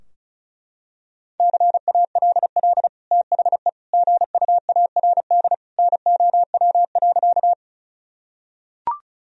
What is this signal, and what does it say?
Morse code, heard as a keyed tone, about 36 wpm, 700 Hz. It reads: CALL THE GUARD NOW.
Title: Sentences from Top 1000 Words - Review Part 4 - 35wpm